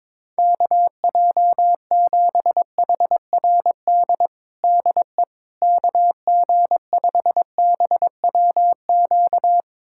Morse code, heard as a keyed tone, about 22 wpm, 700 Hz. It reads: KJ7HRD DE KG5BWQ
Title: USA Callsign DE Callsign 22wpm